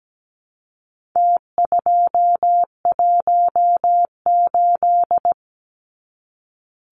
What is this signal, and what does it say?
Morse code, heard as a keyed tone, about 17 wpm, 700 Hz. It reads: T218
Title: Numbers - Contest Style - 4-Digit 17wpm